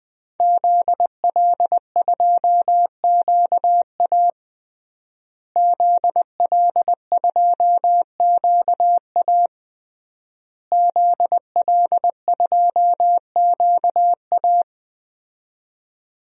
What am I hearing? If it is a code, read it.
Morse code, heard as a keyed tone, about 21 wpm, 700 Hz. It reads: ZL2QA ZL2QA ZL2QA
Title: DXCC-100 Call Signs - 5 Characters - Repeated 3x - 20wpm